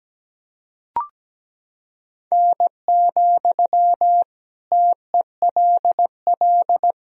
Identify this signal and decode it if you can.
Morse code, heard as a keyed tone, about 17 wpm, 700 Hz. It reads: N, TELL
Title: Sentences from Top 200 Words and Abbreviations 17wpm